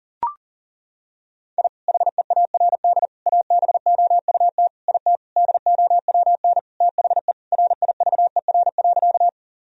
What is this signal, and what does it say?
Morse code, heard as a keyed tone, about 37 wpm, 700 Hz. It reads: I HEARD ABOUT IT DOWN THE RIVER.
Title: Sentences from Top 1000 Words - Review Part 1 - 40wpm